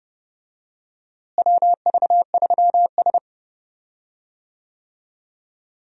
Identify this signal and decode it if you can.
Morse code, heard as a keyed tone, about 30 wpm, 700 Hz. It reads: WV3S
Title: USA Call Signs - 4 Characters - Repeated 3x - 30wpm